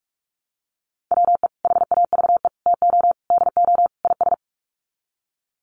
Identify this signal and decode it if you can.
Morse code, heard as a keyed tone, about 44 wpm, 700 Hz. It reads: WE HAVE TO DO IS
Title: Sets of 5 Words Encore 45wpm